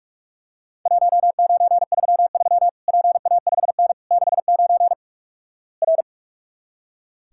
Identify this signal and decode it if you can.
Morse code, heard as a keyed tone, about 46 wpm, 700 Hz. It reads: 1933 PA5N 69 R
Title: Worked All Europe DX Contest - QTC Exchange - 45wpm